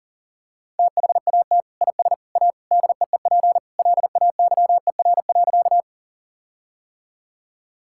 Morse code, heard as a keyed tone, about 37 wpm, 700 Hz. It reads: THAT IS A DEEP LAYER.
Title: Sentences from Top 1600 Words 40wpm